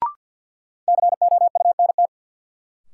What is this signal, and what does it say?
Morse code, heard as a keyed tone, about 51 wpm, 700 Hz. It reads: COUNT
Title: Sentences from Top 1500 Words - New Word Drill - 50wpm